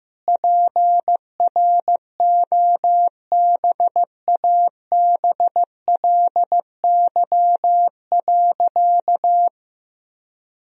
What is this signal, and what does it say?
Morse code, heard as a keyed tone, about 15 wpm, 700 Hz. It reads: PROBABLY.
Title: Sentences from Top 1000 Words - Review Part 4 - 15wpm